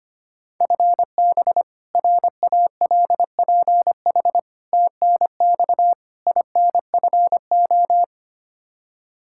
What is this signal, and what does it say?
Morse code, heard as a keyed tone, about 25 wpm, 700 Hz. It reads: FB RALPH TNX INFO
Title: Practice QSO - Exchange 3 - More Info - 25wpm